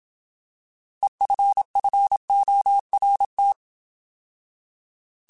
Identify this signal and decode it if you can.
Morse code, heard as a keyed tone, about 27 wpm, 800 Hz. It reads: EFFORT
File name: Alphabetical / Alphabetical mp3s / Es @25